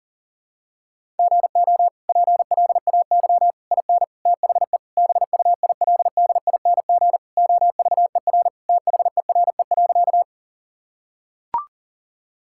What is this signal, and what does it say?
Morse code, heard as a keyed tone, about 40 wpm, 700 Hz. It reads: GO PLAY IN THE BUILDING OVER THERE.
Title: Sentences from Top 1000 Words - Review Part 5 - 40wpm